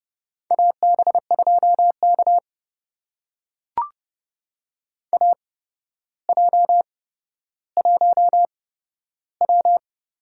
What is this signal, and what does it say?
Morse code, heard as a keyed tone, about 30 wpm, spaced as for 13 wpm, 700 Hz. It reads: AB2K AJ1W
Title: USA Call Signs - 4 Characters - 8x Character Spacing - 30wpm